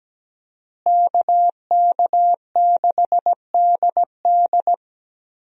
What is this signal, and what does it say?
Morse code, heard as a keyed tone, about 17 wpm, 700 Hz. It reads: KK6DD
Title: USA Call Signs - 5 Characters - Repeated 2x - 17wpm